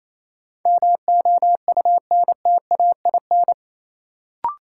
Morse code, heard as a keyed tone, about 27 wpm, 700 Hz. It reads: MOUNTAIN
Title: Sentences from Top 1300 Words - New Word Drill Alt - 28wpm